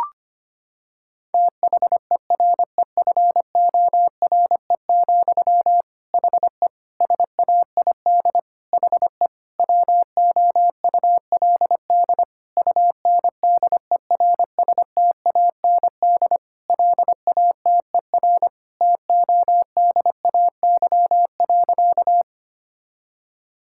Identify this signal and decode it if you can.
Morse code, heard as a keyed tone, about 25 wpm, 700 Hz. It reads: THEREFORE, HE SAID HE WOULD UNDERSTAND LATER TODAY.
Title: Sentences from Top 1000 Words - Review Part 5 - 25wpm